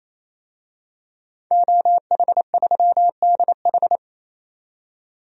Morse code, heard as a keyed tone, about 28 wpm, 700 Hz. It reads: OH3DH
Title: DXCC-100 Call Signs - 5 Characters - 28wpm